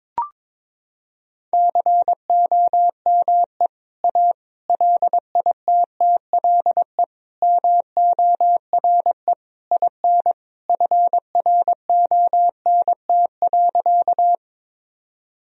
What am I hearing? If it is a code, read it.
Morse code, heard as a keyed tone, about 22 wpm, 700 Hz. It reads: COME A LITTLE MORE IN FRONT.
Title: Sentences from Top 1000 Words - Review Part 2 - 22wpm